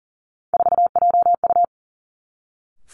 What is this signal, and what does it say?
Morse code, heard as a keyed tone, about 38 wpm, 700 Hz. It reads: 4JU
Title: Three Challenging Letter-Number Combinations 40wpm